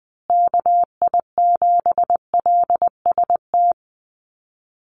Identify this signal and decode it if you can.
Morse code, heard as a keyed tone, about 20 wpm, 700 Hz. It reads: KI7LST
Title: USA Call Signs Repeated 2x 20wpm